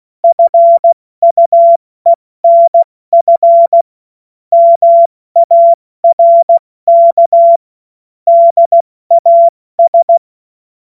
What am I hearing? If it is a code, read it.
Morse code, heard as a keyed tone, about 16 wpm, spaced as for 14 wpm, 680 Hz. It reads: FUENF MARK DAS
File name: Morsekurs / Witze / Witz 05 März 2024 70 F80